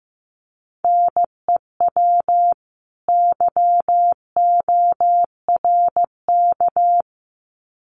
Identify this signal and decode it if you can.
Morse code, heard as a keyed tone, about 15 wpm, 700 Hz. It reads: NEW YORK
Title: Sets of 2 Words 15wpm